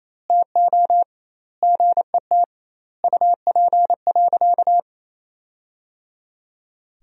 Morse code, heard as a keyed tone, about 27 wpm, 700 Hz. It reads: TO GET UP.
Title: 2x Word-Spacing - Sentences from Top 500 Words - 28wpm